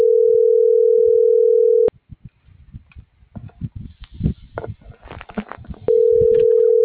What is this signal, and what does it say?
Ringback tone (North America), 440 + 480 Hz.